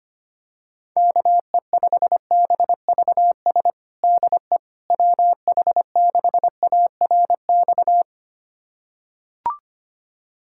Morse code, heard as a keyed tone, about 25 wpm, 700 Hz. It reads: KE5BVS DE WH6ARX
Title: USA Callsign DE Callsign - 6 Characters - 25wpm